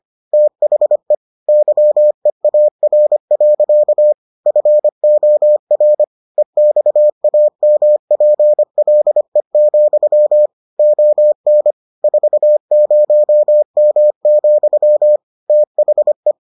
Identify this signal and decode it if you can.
Morse code, heard as a keyed tone, about 25 wpm, 600 Hz. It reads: THE YEAR. FOR EXAMPLE, ON 40M, THE